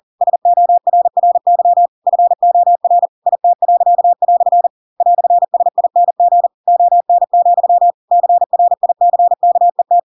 Morse code, heard as a keyed tone, about 40 wpm, 700 Hz. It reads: SORRY FOR IT." "SING ON, CRICKET
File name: Pinocchio / Pinocchio ch4 40 40 ik1hge